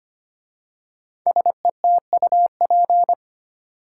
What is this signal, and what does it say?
Morse code, heard as a keyed tone, about 25 wpm, 700 Hz. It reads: SETUP
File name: 5 Letter Words - First 3 Letters Given - 25wpm